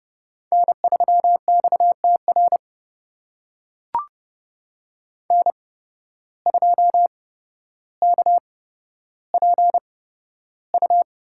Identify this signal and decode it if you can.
Morse code, heard as a keyed tone, about 30 wpm, spaced as for 13 wpm, 700 Hz. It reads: N3XTR N2KPU
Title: USA Call Signs - 5 Characters - 8x Character Spacing - 30wpm